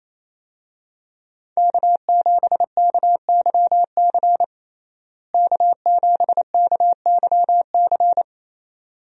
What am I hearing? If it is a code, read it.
Morse code, heard as a keyed tone, about 28 wpm, 700 Hz. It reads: K7KYC K7KYC
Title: USA Call Signs - 5 Characters - Repeated 2x - 28wpm